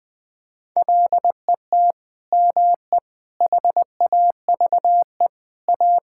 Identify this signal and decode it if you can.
Morse code, heard as a keyed tone, about 20 wpm, 700 Hz. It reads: LET ME HAVE A
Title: Sentences from Top 1000 Words - Review Part 6 - 20wpm